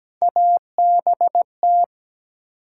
Morse code, heard as a keyed tone, about 17 wpm, 700 Hz. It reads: ABT
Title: Top 90 QSO Elements 17wpm